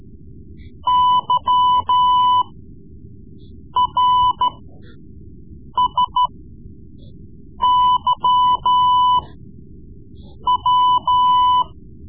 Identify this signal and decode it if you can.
Morse code, heard as a keyed tone, about 14 wpm, spaced as for 5 wpm, 1000 Hz. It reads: YRSYW
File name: Morse Lessons / Morse Lessons / Tape-3B